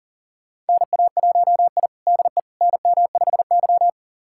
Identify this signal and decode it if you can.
Morse code, heard as a keyed tone, about 40 wpm, 700 Hz. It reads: NA1I DE NM5Y
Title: USA Callsign DE Callsign - 4 Characters - 40wpm